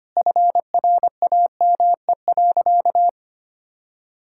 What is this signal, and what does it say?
Morse code, heard as a keyed tone, about 25 wpm, 700 Hz. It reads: FRAME.
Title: Sentences from Top 1000 Words - Review Part 6 - 25wpm